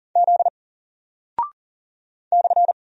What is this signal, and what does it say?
Morse code, heard as a keyed tone, about 40 wpm, 700 Hz. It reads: Z /
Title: Single Character (Letter:Number:Punctuation:Prosign) Alt 40wpm